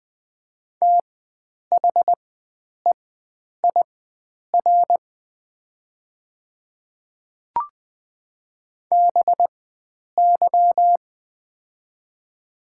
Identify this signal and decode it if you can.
Morse code, heard as a keyed tone, about 20 wpm, spaced as for 9 wpm, 690 Hz. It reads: THEIR BY
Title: 4x Character Spacing - Top 100 Words - 20wpm